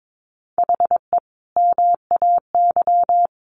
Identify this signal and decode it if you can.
Morse code, heard as a keyed tone, about 22 wpm, 700 Hz. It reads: HE MAY
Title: Sentences from Top 1500 Words 22wpm